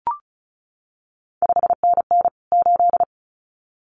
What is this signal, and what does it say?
Morse code, heard as a keyed tone, about 36 wpm, 700 Hz. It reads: 5NN 8
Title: CQ WW DX Contest Exchange - 35wpm